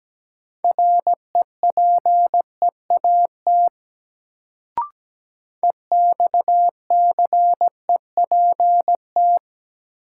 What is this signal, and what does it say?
Morse code, heard as a keyed tone, about 17 wpm, 700 Hz. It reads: REPEAT EXCEPT
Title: Sentences from Top 1600 Words - New Word Drill - 17wpm